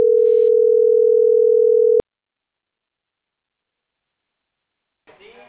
Ringback tone (North America), 440 + 480 Hz.